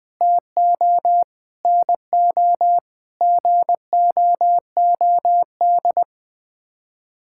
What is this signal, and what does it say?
Morse code, heard as a keyed tone, about 21 wpm, 700 Hz. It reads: TO NO GOOD
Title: Sets of 9 Words - 20wpm